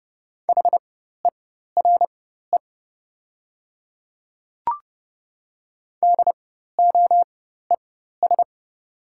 Morse code, heard as a keyed tone, about 30 wpm, spaced as for 14 wpm, 700 Hz. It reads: HERE DOES
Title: 4x Character Spacing - Top 200 Words - 30wpm